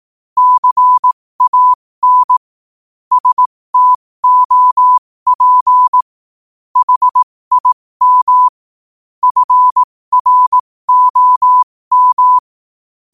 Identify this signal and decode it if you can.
Morse code, heard as a keyed tone, about 18 wpm, spaced as for 15 wpm, 1000 Hz. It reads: CAN STOP HIM FROM
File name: qotdfiles / 15wpm